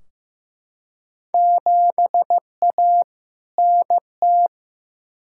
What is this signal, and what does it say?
Morse code, heard as a keyed tone, about 15 wpm, 700 Hz. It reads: 7A NT
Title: Field Day Exchanges Encore - 15wpm